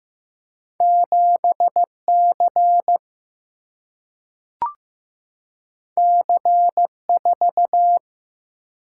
Morse code, heard as a keyed tone, about 15 wpm, 700 Hz. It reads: 7C C4
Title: Single-Letter Single-Number 15wpm